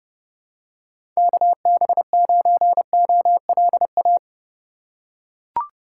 Morse code, heard as a keyed tone, about 30 wpm, 700 Hz. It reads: KB9OLA
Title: USA Call Signs - 6 Characters - 2x Character Spacing - 30wpm